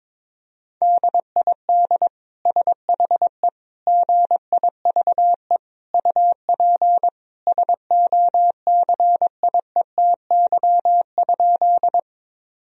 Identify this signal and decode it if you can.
Morse code, heard as a keyed tone, about 22 wpm, 700 Hz. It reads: DID SHE GIVE UP SOCIETY?
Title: Sentences from Top 1000 Words - Review Part 4 - 22wpm